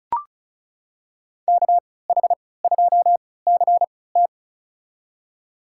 Morse code, heard as a keyed tone, about 36 wpm, spaced as for 20 wpm, 700 Hz. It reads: KH2CT
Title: USA Call Signs - 5 Characters - 3x Character Spacing - 35wpm